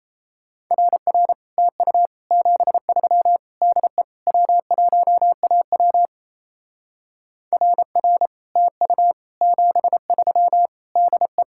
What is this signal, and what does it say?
Morse code, heard as a keyed tone, about 35 wpm, 700 Hz. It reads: RR TU 73 DE W1AW RR TU 73 DE
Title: Speed-Racing Common QSO Phrases Encore - 28wpm